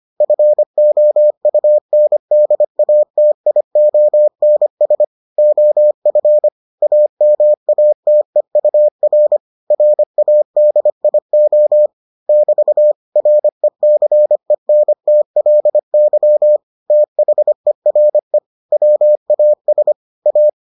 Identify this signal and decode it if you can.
Morse code, heard as a keyed tone, about 25 wpm, 600 Hz. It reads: FOUNDATIONS OF AMATEUR RADIO <BT> RECENTLY THERE WAS A